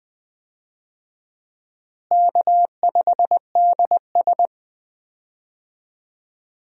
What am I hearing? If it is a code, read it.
Morse code, heard as a keyed tone, about 20 wpm, 700 Hz. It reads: K5DS
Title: USA Call Signs - 4 Characters - Repeated 3x - 20wpm